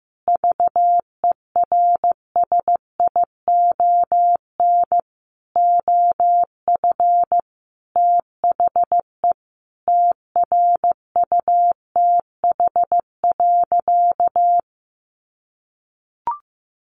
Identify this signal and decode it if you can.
Morse code, heard as a keyed tone, about 15 wpm, 700 Hz. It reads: VERSION OF THE TRUTH.
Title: Sentences from Top 1400 Words 15wpm